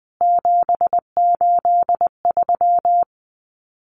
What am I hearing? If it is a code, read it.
Morse code, heard as a keyed tone, about 20 wpm, 700 Hz. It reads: 783
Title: CWT Exchange 20wpm